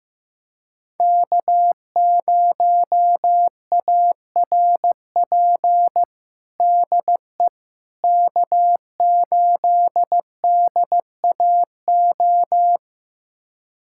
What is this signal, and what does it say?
Morse code, heard as a keyed tone, about 15 wpm, 700 Hz. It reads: K0ARP DE K8DAO
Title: USA Callsign DE Callsign - 5 Characters - 15wpm